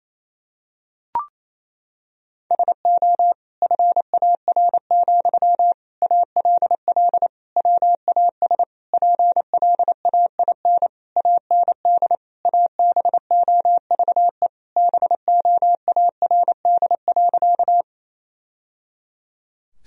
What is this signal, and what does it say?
Morse code, heard as a keyed tone, about 27 wpm, 700 Hz. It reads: SO FAR, ALL WAS PLAIN AND ABOVE BOARD.
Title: Sentences from Top 1000 Words - Review Part 4 - 28wpm